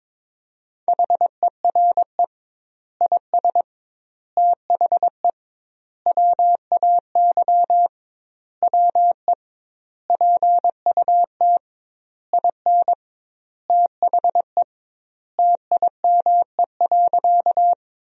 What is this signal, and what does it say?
Morse code, heard as a keyed tone, about 22 wpm, 700 Hz. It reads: HERE IS THE WAY WE PUT IN THE TIME.
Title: 2x Word-Spacing - Sentences from Top 200 Words - 22wpm